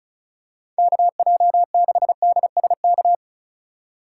Morse code, heard as a keyed tone, about 36 wpm, 700 Hz. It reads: KJ6DSK
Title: USA Call Signs - 6 Characters - 3x Character Spacing - 35wpm